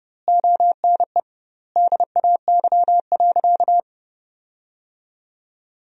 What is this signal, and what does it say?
Morse code, heard as a keyed tone, about 30 wpm, 700 Hz. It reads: ONE DAY.
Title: 2x Word-Spacing - Sentences from Top 300 Words - 30wpm